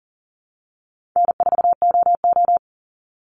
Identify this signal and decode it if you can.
Morse code, heard as a keyed tone, about 40 wpm, 700 Hz. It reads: N4OO
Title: Call Signs - 40wpm